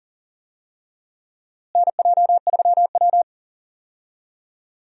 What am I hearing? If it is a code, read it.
Morse code, heard as a keyed tone, about 40 wpm, 700 Hz. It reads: NJ3W
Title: USA Call Signs - 4 Characters - Repeated 2x - 40wpm